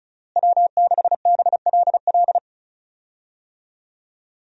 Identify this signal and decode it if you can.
Morse code, heard as a keyed tone, about 34 wpm, 700 Hz. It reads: W6BLL
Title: USA Call Signs - 5 Characters - Repeated 2x - 35wpm